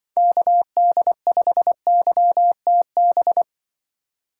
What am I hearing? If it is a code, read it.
Morse code, heard as a keyed tone, about 25 wpm, 700 Hz. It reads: KD5YTB